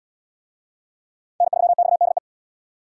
Morse code, heard as a keyed tone, about 78 wpm, 700 Hz. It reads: N3XDE